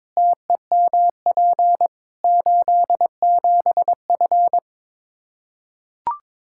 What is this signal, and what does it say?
Morse code, heard as a keyed tone, about 22 wpm, 700 Hz. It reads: TEMP 87F
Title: Practice QSO - Exchange 2 - Info - 22wpm